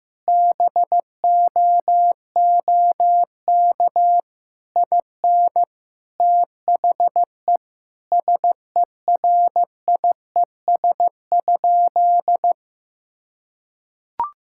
Morse code, heard as a keyed tone, about 16 wpm, 700 Hz. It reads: BOOK IN THE SERIES?
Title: Sentences from Top 1000 Words - Review Part 5 - 15wpm